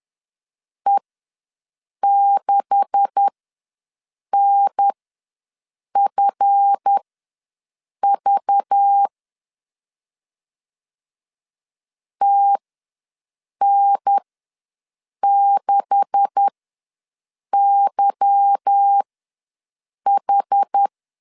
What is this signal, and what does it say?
Morse code, heard as a keyed tone, about 10 wpm, spaced as for 6 wpm, 800 Hz. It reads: E6NFV TN6YH
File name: LEZIONI / 9 286 ws 10wpm 4032